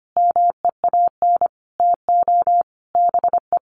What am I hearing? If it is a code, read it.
Morse code, heard as a keyed tone, about 25 wpm, 700 Hz. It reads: MEAN TO BE
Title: Sentences from Top 1400 Words 25wpm